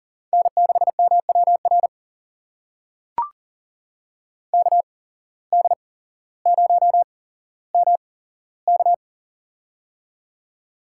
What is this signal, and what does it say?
Morse code, heard as a keyed tone, about 40 wpm, spaced as for 17 wpm, 700 Hz. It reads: N6MWR KD0MK
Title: USA Call Signs - 5 Characters - 8x Character Spacing - 40wpm